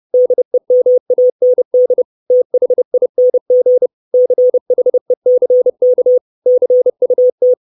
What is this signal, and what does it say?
Morse code, heard as a keyed tone, about 30 wpm, 500 Hz. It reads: DEMAND THING CHECK CUT
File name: 632 Words 30WPM